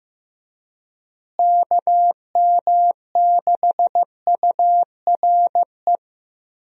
Morse code, heard as a keyed tone, about 15 wpm, 700 Hz. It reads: KM6URE
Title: USA Call Signs - 6 Characters - Repeated 3x - 15wpm